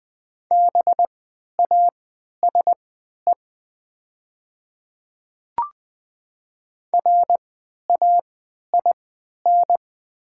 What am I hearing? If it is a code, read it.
Morse code, heard as a keyed tone, about 20 wpm, spaced as for 11 wpm, 700 Hz. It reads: BASE RAIN
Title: 3x Character Spacing - Top 500 Words - 20wpm